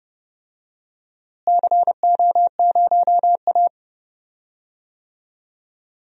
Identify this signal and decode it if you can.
Morse code, heard as a keyed tone, about 30 wpm, 700 Hz. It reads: CO0A
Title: DXCC-100 Call Signs - 4 Characters - Repeated 2x - 30wpm